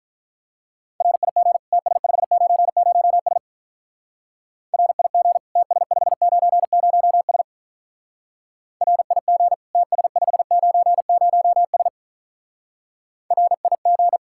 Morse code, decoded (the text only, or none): RIG TS590S RIG TS590S RIG TS590S RIG